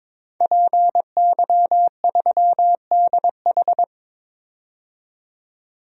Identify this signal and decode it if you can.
Morse code, heard as a keyed tone, about 22 wpm, 700 Hz. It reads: PY3DH
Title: DXCC-100 Call Signs - 5 Characters - 22wpm